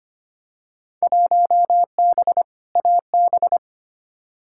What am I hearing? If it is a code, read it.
Morse code, heard as a keyed tone, about 25 wpm, 700 Hz. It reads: 1B AB
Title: Field Day Exchanges - 25wpm